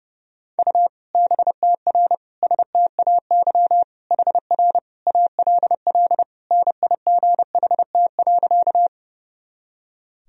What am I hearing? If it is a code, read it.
Morse code, heard as a keyed tone, about 31 wpm, 700 Hz. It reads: U BTR STAY HR ALL NIGHT.
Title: Sentences from Top 500 Words and Abbreviations 30wpm